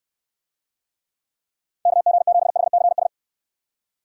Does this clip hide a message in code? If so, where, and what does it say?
Yes: Morse code from 1.9 to 3.1 s, heard as a keyed tone, about 69 wpm, 700 Hz. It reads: KG7HCS